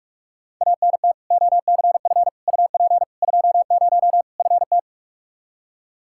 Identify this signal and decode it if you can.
Morse code, heard as a keyed tone, about 46 wpm, 700 Hz. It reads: ANT OCF UP 20 FT
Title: K1USN SST QSO Practice 45wpm